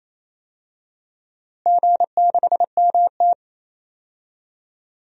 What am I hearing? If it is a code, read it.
Morse code, heard as a keyed tone, about 28 wpm, 700 Hz. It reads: G6MT
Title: DXCC-100 Call Signs - 4 Characters - Repeated 3x - 28wpm